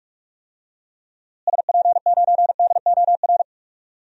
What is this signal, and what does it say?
Morse code, heard as a keyed tone, about 47 wpm, 700 Hz. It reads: SP9DOR
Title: DXCC-100 Call Signs - 6 Characters - Repeated 3x - 45wpm